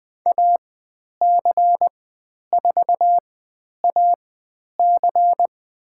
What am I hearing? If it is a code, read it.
Morse code, heard as a keyed tone, about 20 wpm, spaced as for 10 wpm, 700 Hz. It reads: AC4AC
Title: 109 Lesson 18g - Letter m - Call Signs 20wpm